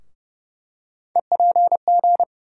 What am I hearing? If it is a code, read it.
Morse code, heard as a keyed tone, about 30 wpm, 700 Hz. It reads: EPG
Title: USA Call Sign Postfixes - 3 Letters 30wpm